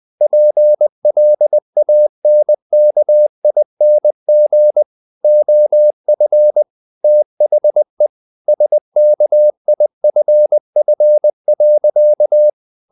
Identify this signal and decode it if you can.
Morse code, heard as a keyed tone, about 20 wpm, 600 Hz. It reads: PLANKING OF THE SKIFF.
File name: the old man and the sea part 2 20wpm